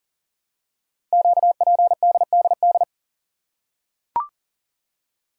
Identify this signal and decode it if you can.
Morse code, heard as a keyed tone, about 41 wpm, 700 Hz. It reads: QPDDD
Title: Code Groups - 5 Letters - 40wpm